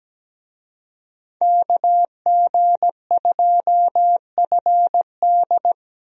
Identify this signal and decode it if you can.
Morse code, heard as a keyed tone, about 17 wpm, 700 Hz. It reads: KG2FD
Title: USA Call Signs - 5 Characters - Repeated 2x - 17wpm